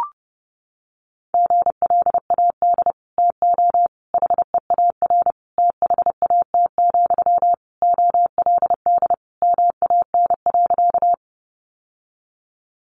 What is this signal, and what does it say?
Morse code, heard as a keyed tone, about 30 wpm, 700 Hz. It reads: GLAD TO HEAR THAT, OLD MAN.
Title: Sentences from Top 800 Words 30wpm